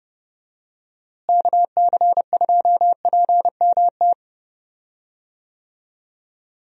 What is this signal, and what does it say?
Morse code, heard as a keyed tone, about 30 wpm, 700 Hz. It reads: KC2PMT
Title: USA Call Signs - 6 Characters - Repeated 2x - 30wpm